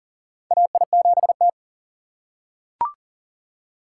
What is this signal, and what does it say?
Morse code, heard as a keyed tone, about 40 wpm, 700 Hz. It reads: AI7T
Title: USA Call Signs - 4x Character Spacing - 40wpm